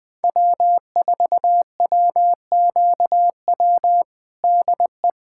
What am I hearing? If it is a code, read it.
Morse code, heard as a keyed tone, about 20 wpm, 700 Hz. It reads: W4WQW DE
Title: USA Callsign DE Callsign - 5 Characters - 20wpm